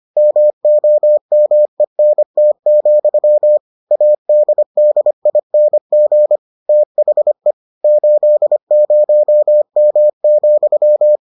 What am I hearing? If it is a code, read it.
Morse code, heard as a keyed tone, about 25 wpm, 600 Hz. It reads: MOMENT, ADDING THE 80M,